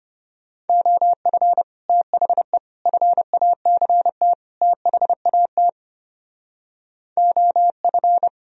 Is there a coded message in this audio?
Yes: Morse code from 0.7 to 8.3 s, heard as a keyed tone, about 33 wpm, 700 Hz. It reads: OF THE FACT THAT OF